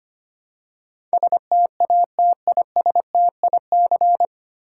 Morse code, heard as a keyed tone, about 25 wpm, 700 Hz. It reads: STATISTIC
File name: practice / 25wpm / 9 Letter Words - First 3 Letters Given - 25wpm